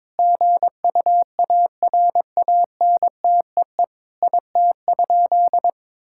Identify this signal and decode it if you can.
Morse code, heard as a keyed tone, about 22 wpm, 700 Hz. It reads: GUARANTEE IT?
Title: Sentences from Top 1600 Words 22wpm